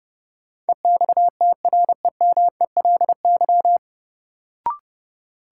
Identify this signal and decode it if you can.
Morse code, heard as a keyed tone, about 32 wpm, 700 Hz. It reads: EXTREMELY
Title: Sentences from Top 1200 Words - New Word Drill Alt - 30wpm